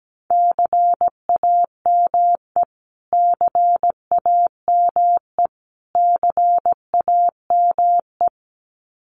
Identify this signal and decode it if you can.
Morse code, heard as a keyed tone, about 17 wpm, 700 Hz. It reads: CAME CAME CAME
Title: Top 500 Words Repeated 17wpm